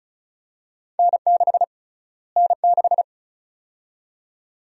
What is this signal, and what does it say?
Morse code, heard as a keyed tone, about 36 wpm, 700 Hz. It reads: N6 N6
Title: USA Call Sign Prefixes - Repeated 2x - 35wpm